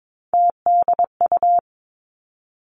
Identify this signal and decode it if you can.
Morse code, heard as a keyed tone, about 22 wpm, 700 Hz. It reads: TDU
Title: Three Challenging Letter-Number Combinations 22wpm